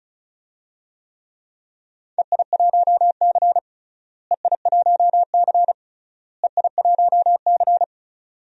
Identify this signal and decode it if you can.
Morse code, heard as a keyed tone, about 34 wpm, 700 Hz. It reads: EI1C EI1C EI1C
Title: DXCC-100 Call Signs - 4 Characters - Repeated 3x - 35wpm